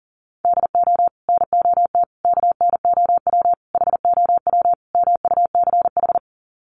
Morse code, heard as a keyed tone, about 38 wpm, 700 Hz. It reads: DO NOT KNOW HOW MUCH
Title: Sets of 5 Words Encore 40wpm